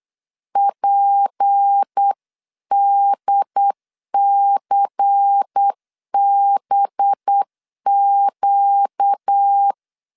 Morse code, heard as a keyed tone, about 8 wpm, 800 Hz. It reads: PDCBQ
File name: CW / DET / 1 6 1500C S B 8wpm 5142